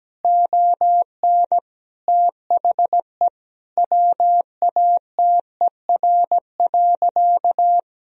Morse code, heard as a keyed tone, about 17 wpm, 700 Hz. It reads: ON THE WATER.
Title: Sentences from Top 500 Words and Abbreviations 17wpm